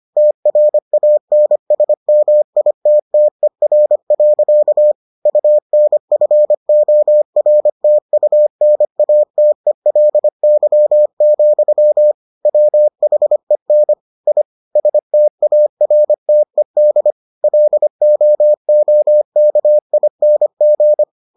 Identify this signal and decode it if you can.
Morse code, heard as a keyed tone, about 25 wpm, 600 Hz. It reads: TRANSMITTER. UNFORTUNATELY, WHEN I STARTED LOOKING